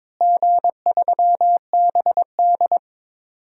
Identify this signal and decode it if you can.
Morse code, heard as a keyed tone, about 22 wpm, 700 Hz. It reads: G3BD
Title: DXCC-100 Call Signs - 4 Characters - Repeated 2x - 22wpm